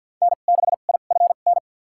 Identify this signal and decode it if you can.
Morse code, heard as a keyed tone, about 51 wpm, spaced as for 36 wpm, 700 Hz. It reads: N6IFN
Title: USA Call Signs - 5 Characters - 2x Character Spacing - 50wpm